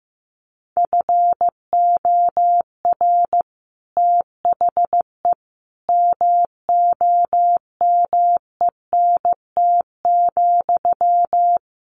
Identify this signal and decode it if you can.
Morse code, heard as a keyed tone, about 15 wpm, 700 Hz. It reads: FOR THE MOMENT,
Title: Sentences from Top 1400 Words 15wpm